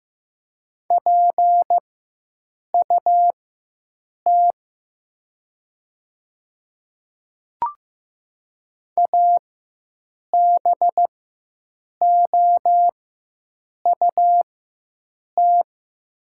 Morse code, heard as a keyed tone, about 15 wpm, spaced as for 7 wpm, 700 Hz. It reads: PUT ABOUT
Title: 4x Character Spacing - Top 300 Words - 15wpm